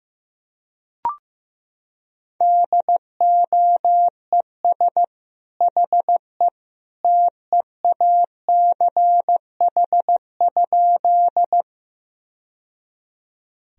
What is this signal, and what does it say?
Morse code, heard as a keyed tone, about 15 wpm, 700 Hz. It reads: DOES HE TEACH?
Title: Sentences from Top 1000 Words - Review Part 2 - 15wpm